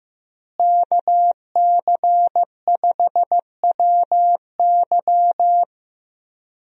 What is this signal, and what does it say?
Morse code, heard as a keyed tone, about 15 wpm, 700 Hz. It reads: KC5WY
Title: USA Call Signs - 5 Characters - Repeated 3x - 15wpm